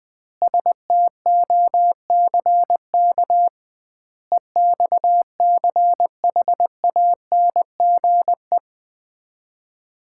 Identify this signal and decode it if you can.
Morse code, heard as a keyed tone, about 20 wpm, 700 Hz. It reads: STOCK EXCHANGE
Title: 2x Word-Spacing - Sets of 5 Words - 20wpm